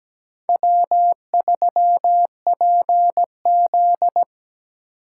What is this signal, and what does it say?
Morse code, heard as a keyed tone, about 20 wpm, 700 Hz. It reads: W3PZ
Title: Speed Racing - USA 1x2 Call Signs 17wpm